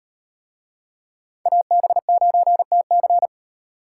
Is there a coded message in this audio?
Yes: Morse code from 1.5 to 3.3 s, heard as a keyed tone, about 38 wpm, 700 Hz. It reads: AB9TC